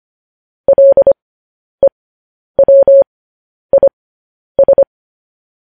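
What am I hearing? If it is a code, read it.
Morse code, heard as a keyed tone, about 25 wpm, spaced as for 10 wpm, 550 Hz. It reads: LEWIS
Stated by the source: Morse code news headlines sent as a reading transcription, preceded by vvv.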